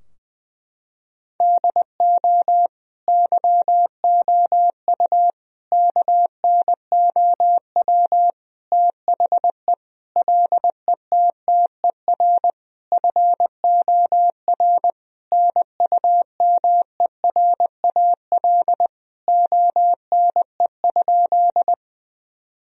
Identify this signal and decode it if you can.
Morse code, heard as a keyed tone, about 20 wpm, 700 Hz. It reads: DO YOU KNOW THE LETTER FOR NUMERAL ONE?
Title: Sentences from Top 1000 Words - Review Part 2 - 20wpm